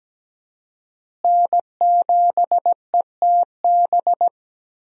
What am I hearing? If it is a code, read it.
Morse code, heard as a keyed tone, about 17 wpm, 700 Hz. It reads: N7ETB
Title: USA Call Signs - 5 Characters - Repeated 2x - 17wpm